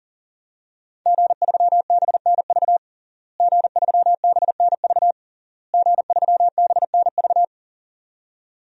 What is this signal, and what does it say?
Morse code, heard as a keyed tone, about 40 wpm, 700 Hz. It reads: G3BNV G3BNV G3BNV
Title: DXCC-100 Call Signs - 5 Characters - Repeated 3x - 40wpm